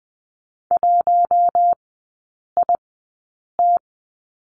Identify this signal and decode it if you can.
Morse code, heard as a keyed tone, about 20 wpm, spaced as for 8 wpm, 700 Hz. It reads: 1IT
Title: 041 Lesson 8e - Number 1 - Three Characters 20wpm